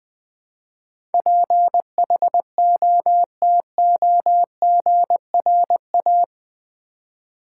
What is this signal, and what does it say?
Morse code, heard as a keyed tone, about 19 wpm, 700 Hz. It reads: PHOTOGRA
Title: Build A Word - 4 Letters At A Time - 12-Letter Words - 20wpm